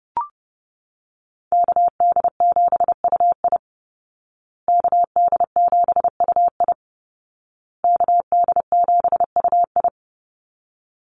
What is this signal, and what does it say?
Morse code, heard as a keyed tone, about 28 wpm, 700 Hz. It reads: KD7UI KD7UI KD7UI
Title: Call Signs Repeated 2x - 30wpm